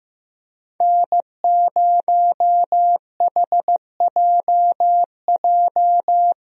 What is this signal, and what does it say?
Morse code, heard as a keyed tone, about 16 wpm, 700 Hz. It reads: N0HJJ
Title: USA Call Signs - 5 Characters - Repeated 2x - 15wpm